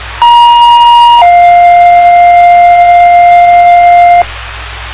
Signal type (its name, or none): two-tone page